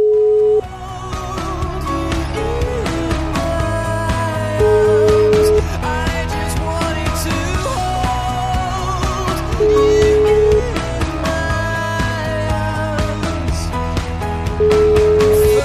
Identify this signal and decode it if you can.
Ringback tone (Europe, 425 Hz), with noise.